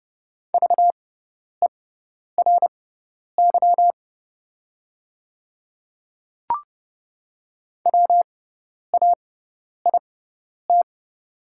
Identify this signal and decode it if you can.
Morse code, heard as a keyed tone, about 30 wpm, spaced as for 10 wpm, 690 Hz. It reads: VERY WAIT
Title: 6x Character Spacing - Top 500 Words - 30wpm